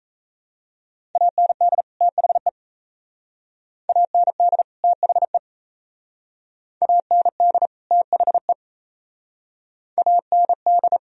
Morse code, heard as a keyed tone, about 35 wpm, 700 Hz. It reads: AND THE AND THE AND THE AND
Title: Speed-Racing - Sets of 2 Words 28wpm